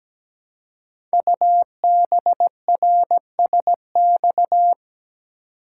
Morse code, heard as a keyed tone, about 17 wpm, 700 Hz. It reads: UBRSX
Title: Code Groups - 5 Letters - 17wpm